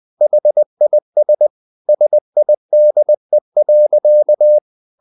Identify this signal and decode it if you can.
Morse code, heard as a keyed tone, about 20 wpm, 600 Hz. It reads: HIS SIDE.
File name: the old man and the sea part 2 20wpm